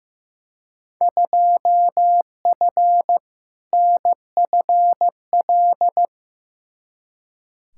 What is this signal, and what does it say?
Morse code, heard as a keyed tone, about 15 wpm, 700 Hz. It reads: 2F NFL
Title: Field Day Exchanges - 15wpm